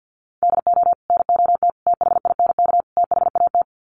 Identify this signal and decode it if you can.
Morse code, heard as a keyed tone, about 50 wpm, 700 Hz. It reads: DO NOT THINK THAT
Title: Sets of 4 Words 50wpm